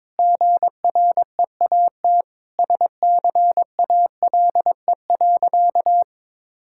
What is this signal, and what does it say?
Morse code, heard as a keyed tone, about 22 wpm, 700 Hz. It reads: GREAT SCALE.
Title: Sentences from Top 1000 Words - Review Part 5 - 22wpm